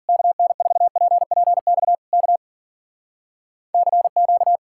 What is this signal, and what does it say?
Morse code, heard as a keyed tone, about 52 wpm, 700 Hz. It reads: KN4PPX K CQ